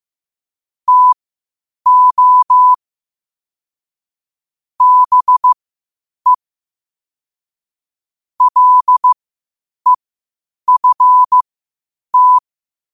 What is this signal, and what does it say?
Morse code, heard as a keyed tone, about 15 wpm, spaced as for 8 wpm, 1000 Hz. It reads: TO BE LEFT
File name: qotdfiles / 8wpm